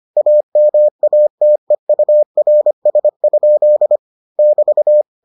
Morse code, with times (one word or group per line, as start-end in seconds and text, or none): 0.2-4.0 AMATEURS?
4.4-5.0 <BT>